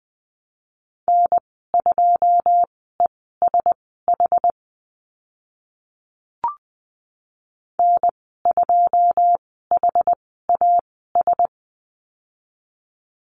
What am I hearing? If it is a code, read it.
Morse code, heard as a keyed tone, about 20 wpm, spaced as for 14 wpm, 700 Hz. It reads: N2ESH N2HAS
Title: 082 Lesson 14g - Number 2 - Call Signs 20wpm